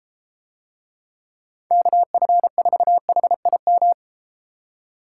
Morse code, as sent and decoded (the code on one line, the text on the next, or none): -.- ..-. ....- .... .. --
KF4HIM